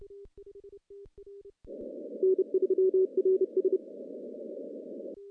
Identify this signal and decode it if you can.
Morse code, heard as a keyed tone, about 30 wpm, 380 Hz. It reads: N3RS
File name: CW-SS